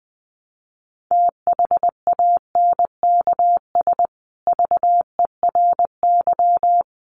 Morse code, heard as a keyed tone, about 20 wpm, 700 Hz. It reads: THANKS VERY
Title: Sets of 5 Words Encore2 20wpm